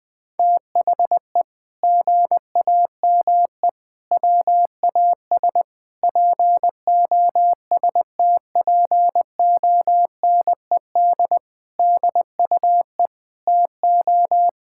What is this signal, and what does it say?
Morse code, heard as a keyed tone, about 20 wpm, 700 Hz. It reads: THE GAME WAS POSTPONED DUE TO
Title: Sentences with 6-Word Phrases - Spoken Beginning - 20wpm